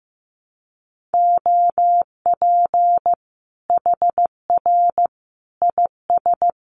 Morse code, heard as a keyed tone, about 15 wpm, 700 Hz. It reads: OP HR IS
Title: Common QSO Phrases 15wpm